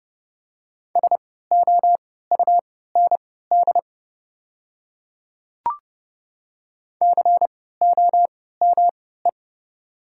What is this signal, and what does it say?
Morse code, heard as a keyed tone, about 30 wpm, spaced as for 17 wpm, 700 Hz. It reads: SOUND COME
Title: 3x Character Spacing - Top 100 Words - 30wpm